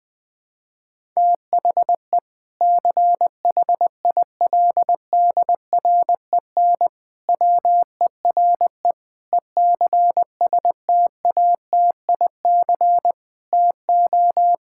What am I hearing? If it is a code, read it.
Morse code, heard as a keyed tone, about 20 wpm, 700 Hz. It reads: THE CHILDREN WERE ECSTATIC TO
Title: Difficult Sentences with Hints - 20wpm